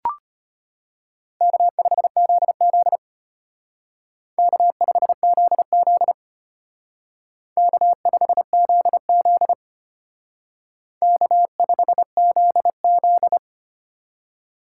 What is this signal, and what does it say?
Morse code, heard as a keyed tone, about 32 wpm, 700 Hz. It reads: K5ZZ K5ZZ K5ZZ K5ZZ